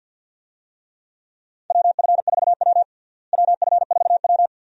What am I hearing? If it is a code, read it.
Morse code, heard as a keyed tone, about 53 wpm, 700 Hz. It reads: WF4W WF4W
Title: USA Call Signs - 4 Characters - Repeated 2x - 50wpm